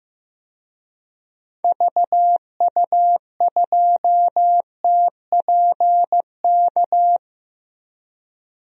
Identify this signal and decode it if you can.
Morse code, heard as a keyed tone, about 15 wpm, 700 Hz. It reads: VU2TPK
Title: DXCC-100 Call Signs - 6 Characters - Repeated 3x - 15wpm